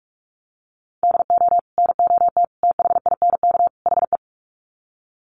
Morse code, heard as a keyed tone, about 44 wpm, 700 Hz. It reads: DO NOT THINK HE
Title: Sets of 4 Words 45wpm